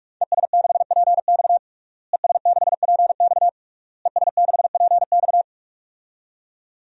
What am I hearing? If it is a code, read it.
Morse code, heard as a keyed tone, about 46 wpm, 700 Hz. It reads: ES6PX ES6PX ES6PX
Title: DXCC-100 Call Signs - 5 Characters - Repeated 3x - 45wpm